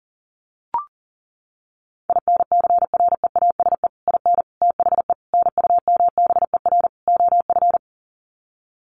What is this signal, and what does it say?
Morse code, heard as a keyed tone, about 40 wpm, 700 Hz. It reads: INCREASE IN THE NUMBER OF
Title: Sets of 5 Words 40wpm